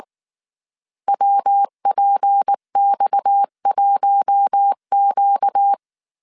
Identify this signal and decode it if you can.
Morse code, heard as a keyed tone, about 19 wpm, 800 Hz. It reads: WPX1Q